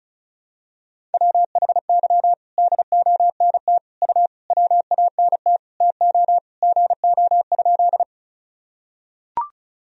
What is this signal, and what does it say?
Morse code, heard as a keyed tone, about 36 wpm, 700 Hz. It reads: WHY DONT U WANT TO GO?
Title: Sentences from Top 300 Words and Abbreviations 35wpm